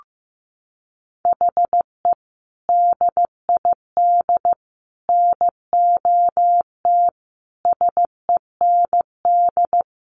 Morse code, heard as a keyed tone, about 15 wpm, 700 Hz. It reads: HE DID NOT SEND
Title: Sentences from Top 1500 Words 15wpm